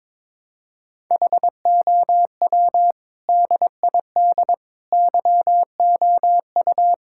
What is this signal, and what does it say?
Morse code, heard as a keyed tone, about 22 wpm, 700 Hz. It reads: HOW DID YOU
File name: practice / 22wpm / Sets of 8 Words - 22wpm